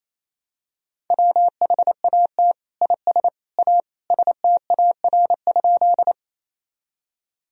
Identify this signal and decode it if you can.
Morse code, heard as a keyed tone, about 28 wpm, 700 Hz. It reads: WHAT IS A STAR?
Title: Sentences from Top 1000 Words - Review Part 2 - 28wpm